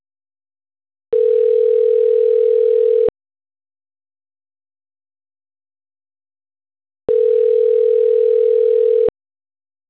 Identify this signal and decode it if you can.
Ringback tone (North America), 440 + 480 Hz.
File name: autotestagent / 10-54-35.136..callcentertransfer. .ed -9587-4020-8519